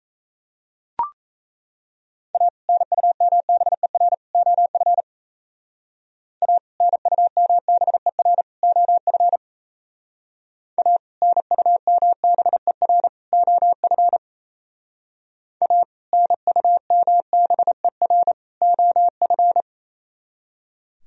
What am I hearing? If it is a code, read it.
Morse code, heard as a keyed tone, about 35 wpm, 700 Hz. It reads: A NUMBER OF A NUMBER OF A NUMBER OF A NUMBER OF
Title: Speed-Racing - Sets of 3 Words 28wpm